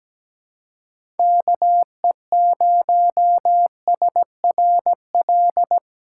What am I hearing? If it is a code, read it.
Morse code, heard as a keyed tone, about 17 wpm, 700 Hz. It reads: KE0SRL
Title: K1USN SST QSO Practice - 17wpm